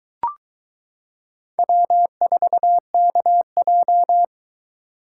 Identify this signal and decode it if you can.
Morse code, heard as a keyed tone, about 22 wpm, 700 Hz. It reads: W4KJ